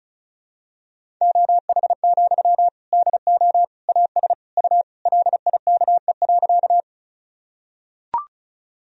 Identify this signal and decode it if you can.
Morse code, heard as a keyed tone, about 34 wpm, 700 Hz. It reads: OH, DO AS U LIKE.
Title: Sentences from Top 500 Words and Abbreviations 35wpm